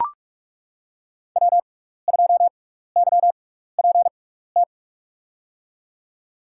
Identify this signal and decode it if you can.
Morse code, heard as a keyed tone, about 42 wpm, spaced as for 15 wpm, 700 Hz. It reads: W2YPT
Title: USA Call Signs - 6x Character Spacing - 45wpm